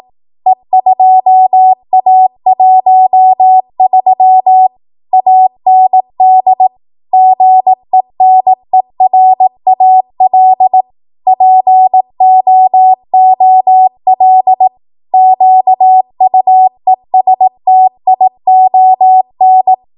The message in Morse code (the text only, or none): E2A13 AND GENERAL POOL QUESTION